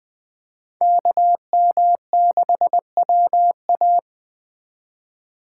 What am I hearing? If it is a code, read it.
Morse code, heard as a keyed tone, about 21 wpm, 700 Hz. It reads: KM6WA